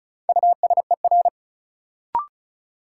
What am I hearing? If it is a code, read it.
Morse code, heard as a keyed tone, about 36 wpm, 700 Hz. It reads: USER
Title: Sentences from Top 1000 Words - Review Part 5 - New Word Drill - 35wpm